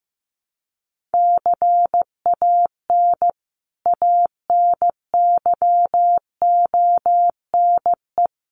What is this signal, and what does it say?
Morse code, heard as a keyed tone, about 15 wpm, 700 Hz. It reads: CAN ANYONE
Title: Sentences from Top 1500 Words 15wpm